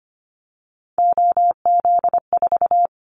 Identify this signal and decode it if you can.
Morse code, heard as a keyed tone, about 25 wpm, 700 Hz. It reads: OZ4
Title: Callsign Prefixes for Top 100 DXCC - 25wpm